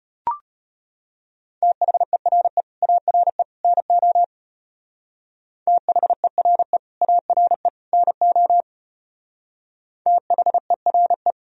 Morse code, heard as a keyed tone, about 30 wpm, 700 Hz. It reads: THERE ARE NO THERE ARE NO THERE
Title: Speed-Racing - Sets of 3 Words 25wpm